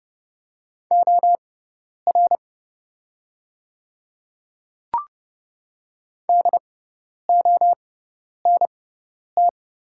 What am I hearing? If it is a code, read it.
Morse code, heard as a keyed tone, about 30 wpm, spaced as for 10 wpm, 700 Hz. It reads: OR DONT
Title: 6x Character Spacing - Top 300 Words - 30wpm